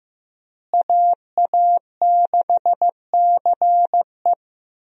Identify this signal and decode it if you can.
Morse code, heard as a keyed tone, about 15 wpm, 700 Hz. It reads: AA6CE
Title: USA Call Signs - 5 Characters - Repeated 3x - 15wpm